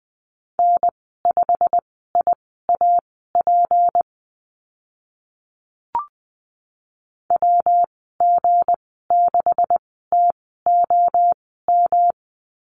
Morse code, heard as a keyed tone, about 20 wpm, spaced as for 14 wpm, 700 Hz. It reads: N5IAP WG6TOM
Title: 173 Lesson 28g - Number 9 - Call Signs 20wpm